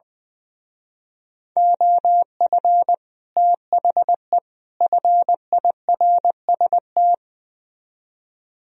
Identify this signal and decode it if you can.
Morse code, heard as a keyed tone, about 21 wpm, 700 Hz. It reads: OF THE FIRST
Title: Speed-Racing - Sets of 3 Words 20wpm